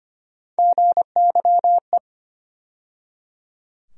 Morse code, heard as a keyed tone, about 25 wpm, 700 Hz. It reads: GYE
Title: USA Call Sign Postfixes - 3 Letters 25wpm